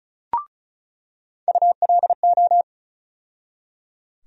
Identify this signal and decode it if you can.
Morse code, heard as a keyed tone, about 34 wpm, 700 Hz. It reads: ULO
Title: USA Call Sign Postfixes - 3 Letters 35wpm